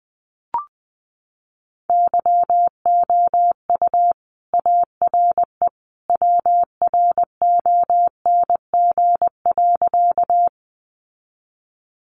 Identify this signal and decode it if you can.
Morse code, heard as a keyed tone, about 20 wpm, 700 Hz. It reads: YOU ARE WRONG.
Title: Sentences from Top 900 Words 20wpm